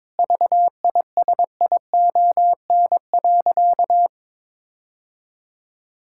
Morse code, heard as a keyed tone, about 22 wpm, 700 Hz. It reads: VISION.
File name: Difficult Sentences with Hints - 22wpm